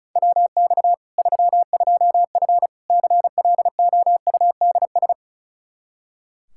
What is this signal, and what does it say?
Morse code, heard as a keyed tone, about 36 wpm, 700 Hz. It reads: WX 32F CLOUDS
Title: Practice QSO - Exchange 2 - Info - 35wpm